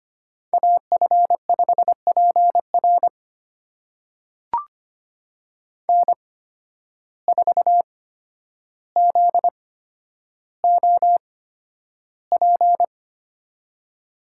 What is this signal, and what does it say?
Morse code, heard as a keyed tone, about 25 wpm, spaced as for 11 wpm, 700 Hz. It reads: AF5PR N4ZOP
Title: USA Call Signs - 5 Characters - 8x Character Spacing - 25wpm